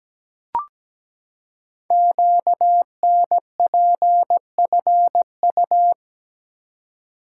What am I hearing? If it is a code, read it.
Morse code, heard as a keyed tone, about 17 wpm, 700 Hz. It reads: QNPFU
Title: Code Groups - 5 Letters - 17wpm